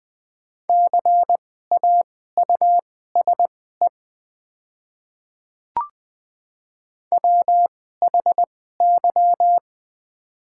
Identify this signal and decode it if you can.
Morse code, heard as a keyed tone, about 20 wpm, spaced as for 14 wpm, 700 Hz. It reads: CAUSE WHY
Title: 2x Character Spacing - Top 200 Words - 20wpm